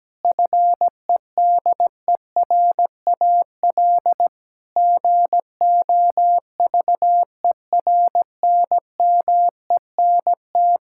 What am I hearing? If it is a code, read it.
Morse code, heard as a keyed tone, about 17 wpm, 700 Hz. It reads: FEDERAL GOVERNMENT
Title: Sentences from Top 1000 Words - Review Part 5 - 17wpm